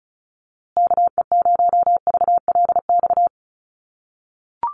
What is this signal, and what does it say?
Morse code, heard as a keyed tone, about 34 wpm, 700 Hz. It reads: KE0VLX
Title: USA Call Signs Repeated 3x 35wpm